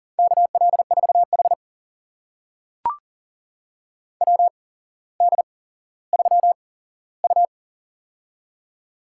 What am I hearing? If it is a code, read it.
Morse code, heard as a keyed tone, about 38 wpm, spaced as for 17 wpm, 700 Hz. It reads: KL4H WD3U
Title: USA Call Signs - 4 Characters - 8x Character Spacing - 40wpm